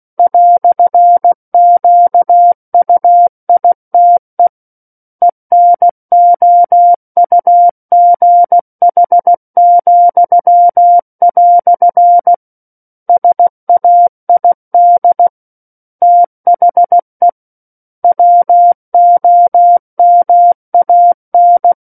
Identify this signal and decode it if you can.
Morse code, heard as a keyed tone, about 16 wpm, 700 Hz. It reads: "QUITE ENOUGH," SAID THE WOMAN
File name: Book One / 07-Chapter 7 - 16 WPM